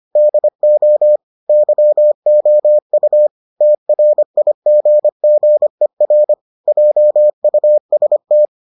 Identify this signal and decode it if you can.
Morse code, heard as a keyed tone, about 25 wpm, 600 Hz. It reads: DO YOU TRIGGER JUST